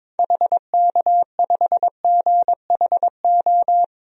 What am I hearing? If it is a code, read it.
Morse code, heard as a keyed tone, about 22 wpm, 700 Hz. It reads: HK5GHO
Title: DXCC-100 Call Signs - 6 Characters - 22wpm